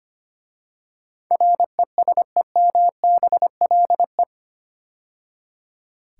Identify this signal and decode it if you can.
Morse code, heard as a keyed tone, about 25 wpm, 700 Hz. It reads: RESEMBLE
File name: 8 Letter Words - First 3 Letters Given - 25wpm